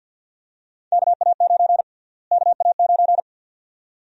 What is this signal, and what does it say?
Morse code, heard as a keyed tone, about 47 wpm, 700 Hz. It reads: KA9 KA9
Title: USA Call Sign Prefixes - Repeated 2x - 50wpm